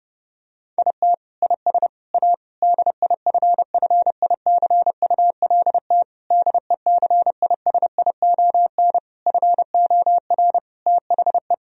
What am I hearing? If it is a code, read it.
Morse code, heard as a keyed tone, about 32 wpm, 700 Hz. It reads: IT IS A DIFFICULT DECISION FOR THE
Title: Sentences from Top 1000 Words - Review Part 4 - 30wpm